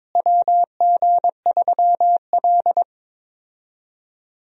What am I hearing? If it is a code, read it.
Morse code, heard as a keyed tone, about 22 wpm, 700 Hz. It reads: WG3L
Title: USA Call Signs - 4 Characters - Repeated 3x - 22wpm